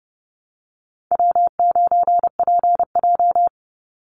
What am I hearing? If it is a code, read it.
Morse code, heard as a keyed tone, about 30 wpm, 700 Hz. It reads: W9PJ
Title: Easy Call Signs Repeated 2x - 30wpm